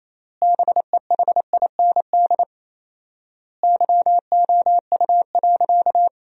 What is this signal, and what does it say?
Morse code, heard as a keyed tone, about 28 wpm, 700 Hz. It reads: BEHIND YOU.
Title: 4x Word-Spacing - Sentences from Top 500 Words - 28wpm